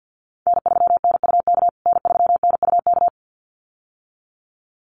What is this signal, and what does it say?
Morse code, heard as a keyed tone, about 50 wpm, 700 Hz. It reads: N3NUK N3NUK
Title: USA Call Signs Repeated 2x 50wpm